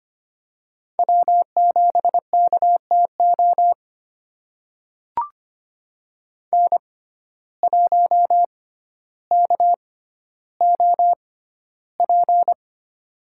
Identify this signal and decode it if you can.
Morse code, heard as a keyed tone, about 25 wpm, spaced as for 13 wpm, 700 Hz. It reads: W7KTO N1KOP
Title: USA Call Signs - 5 Characters - 6x Character Spacing - 25wpm